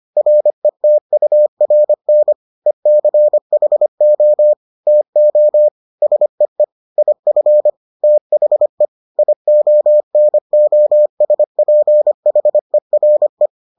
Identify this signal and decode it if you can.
Morse code, heard as a keyed tone, about 25 wpm, 600 Hz. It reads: RETURN ECHO TO SEE IF THE IONOSPHERE